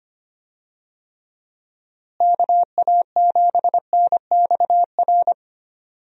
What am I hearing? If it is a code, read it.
Morse code, heard as a keyed tone, about 25 wpm, 700 Hz. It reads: KA7NXR
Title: USA Call Signs - 6 Characters - Repeated 2x - 25wpm